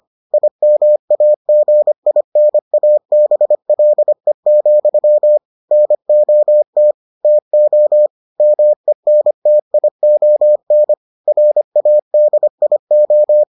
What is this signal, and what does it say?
Morse code, heard as a keyed tone, about 25 wpm, 600 Hz. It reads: IMAGINABLE, NOT TO MENTION RADIO